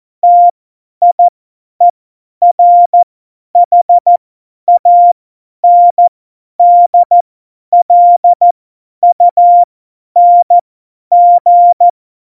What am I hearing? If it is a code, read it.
Morse code, heard as a keyed tone, about 14 wpm, spaced as for 10 wpm, 700 Hz. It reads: TIERHANDLUNG
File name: Morsekurs / Witze / Witz 30. Oktober 2018, 50 F70